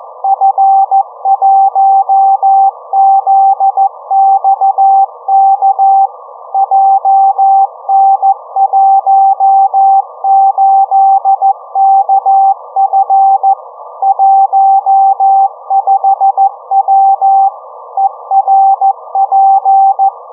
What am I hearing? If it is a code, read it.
Morse code, heard as a keyed tone, about 14 wpm, 780 Hz. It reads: F1ZXK JN18KF 15W ERP